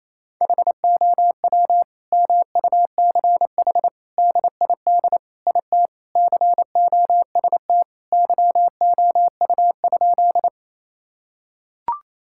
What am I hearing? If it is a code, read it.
Morse code, heard as a keyed tone, about 28 wpm, 700 Hz. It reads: HOW MUCH DID IT COST YOU?
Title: Sentences from Top 1000 Words - Review Part 5 - 28wpm